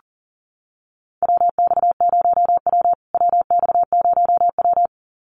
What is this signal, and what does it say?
Morse code, heard as a keyed tone, about 40 wpm, 700 Hz. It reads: WX0W WX0W
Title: USA Call Signs Repeated 2x 40wpm